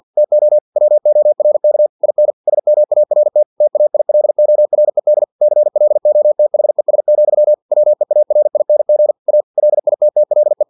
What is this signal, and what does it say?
Morse code, heard as a keyed tone, about 50 wpm, 600 Hz. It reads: TO WORK IN SMART TAILORED CLOTHES, WEARING A LITTLE
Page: Never 49wpm